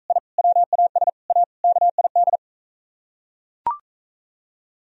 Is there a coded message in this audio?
Yes: Morse code from 0.1 to 2.4 s, heard as a keyed tone, about 45 wpm, 700 Hz. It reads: I WAS A KID